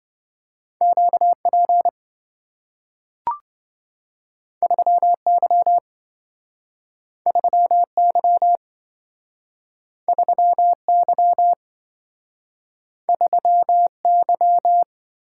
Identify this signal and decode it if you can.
Morse code, heard as a keyed tone, about 27 wpm, 700 Hz. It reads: QP 3Y 3Y 3Y 3Y